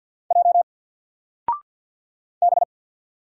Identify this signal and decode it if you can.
Morse code, heard as a keyed tone, about 50 wpm, 700 Hz. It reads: J B
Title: Single Character (Letter:Number:Punctuation:Prosign) 50wpm